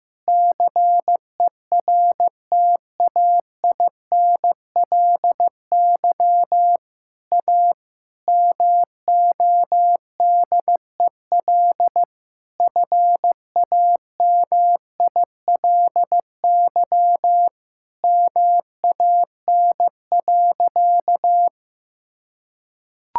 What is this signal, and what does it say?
Morse code, heard as a keyed tone, about 15 wpm, 700 Hz. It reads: CERTAINLY A MODEL FAMILY MAN.
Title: Sentences from Top 1000 Words - Review Part 5 - 15wpm